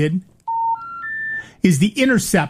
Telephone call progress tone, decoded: special information tone